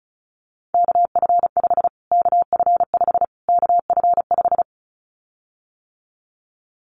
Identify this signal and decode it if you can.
Morse code, heard as a keyed tone, about 34 wpm, 700 Hz. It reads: KF5 KF5 KF5
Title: Callsign Prefixes for Top 25 DXCC - Repeated 3x - 35wpm